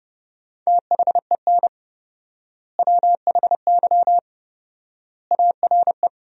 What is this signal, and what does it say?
Morse code, heard as a keyed tone, about 29 wpm, 700 Hz. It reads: THEN WHY ARE
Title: 4x Word-Spacing - Sentences from Top 200 Words - 30wpm